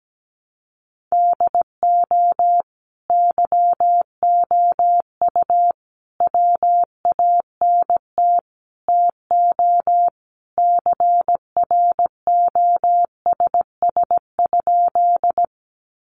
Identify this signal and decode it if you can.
Morse code, heard as a keyed tone, about 17 wpm, 700 Hz. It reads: DO YOU WANT TO CROSS?
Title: Sentences from Top 300 Words 17wpm